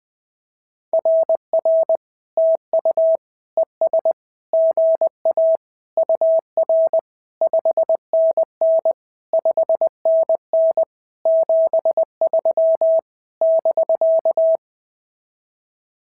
Morse code, heard as a keyed tone, about 20 wpm, 650 Hz. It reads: RR TU ES GA UR 5NN 5NN 73 <BK>
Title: SOTA QSO Practice 20wpm